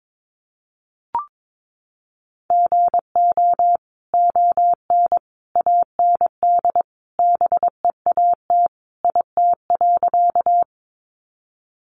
Morse code, heard as a keyed tone, about 22 wpm, 700 Hz. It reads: GO ON AND BEAT IT.
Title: Sentences from Top 1300 Words 22wpm